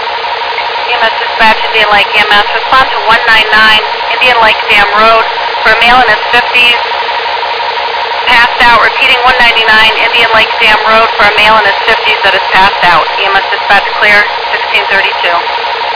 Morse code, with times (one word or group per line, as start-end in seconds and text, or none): none